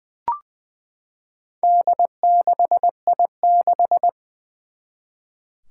Morse code, heard as a keyed tone, about 20 wpm, 700 Hz. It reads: D6I6
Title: Four Challenging Letter-Number Combinations 20wpm